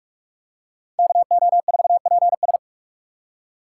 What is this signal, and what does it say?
Morse code, heard as a keyed tone, about 44 wpm, 700 Hz. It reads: KO4PS